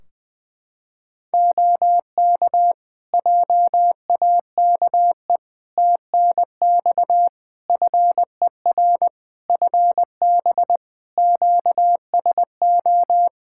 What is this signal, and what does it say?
Morse code, heard as a keyed tone, about 20 wpm, 700 Hz. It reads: OK JAKE TNX FER FB QSO
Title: Practice QSO - Exchange 4 - Ending - 20wpm